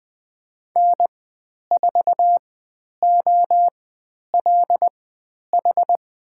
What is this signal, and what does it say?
Morse code, heard as a keyed tone, about 20 wpm, spaced as for 10 wpm, 700 Hz. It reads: N4OLH
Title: 109 Lesson 18g - Letter m - Call Signs 20wpm